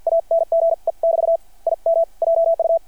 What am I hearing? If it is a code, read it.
Morse code, heard as a keyed tone, about 50 wpm, spaced as for 43 wpm, 660 Hz. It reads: ANGE <BT> IM JU
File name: archived / CRSnet